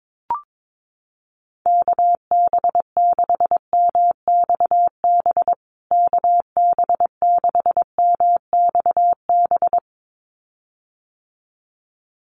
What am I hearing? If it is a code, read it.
Morse code, heard as a keyed tone, about 22 wpm, 700 Hz. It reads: KB6MXB KB6MXB
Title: USA Call Signs Repeated 2x 22wpm